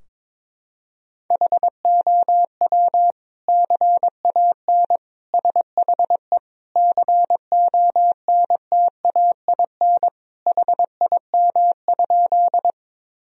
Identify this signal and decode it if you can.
Morse code, heard as a keyed tone, about 22 wpm, 700 Hz. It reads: HOW CAN SHE CONTAIN HIM?
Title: Sentences from Top 1000 Words - Review Part 2 - 22wpm